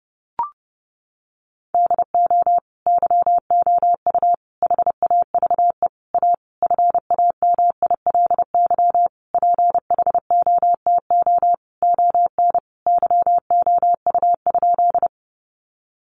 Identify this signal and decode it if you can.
Morse code, heard as a keyed tone, about 30 wpm, 700 Hz. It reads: DO YOU HAVE A FAMILY PHOTO ON YOU?
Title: Sentences from Top 1200 Words 30wpm